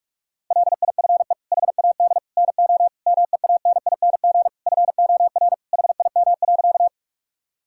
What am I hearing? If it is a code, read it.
Morse code, heard as a keyed tone, about 48 wpm, 700 Hz. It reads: LIFE HAD NO MEANING FOR HIM.
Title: Sentences from Top 1000 Words - Review Part 6 - 45wpm